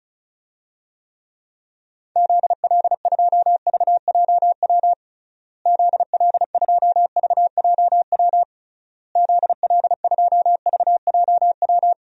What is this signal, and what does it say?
Morse code, heard as a keyed tone, about 36 wpm, 700 Hz. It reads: ZL2VJW ZL2VJW ZL2VJW
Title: DXCC-100 Call Signs - 6 Characters - Repeated 3x - 35wpm